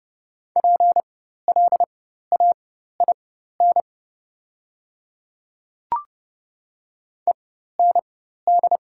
Morse code, heard as a keyed tone, about 31 wpm, spaced as for 14 wpm, 700 Hz. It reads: PLAIN END
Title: 4x Character Spacing - Top 500 Words - 30wpm